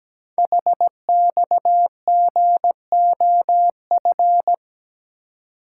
Morse code, heard as a keyed tone, about 17 wpm, 700 Hz. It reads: HXGOF
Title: Code Groups - 5 Letters - 17wpm